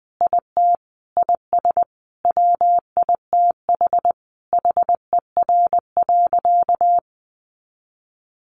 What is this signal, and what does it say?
Morse code, heard as a keyed tone, about 20 wpm, 700 Hz. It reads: IT IS WITH HER.
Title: Sentences from Top 100 Words 20wpm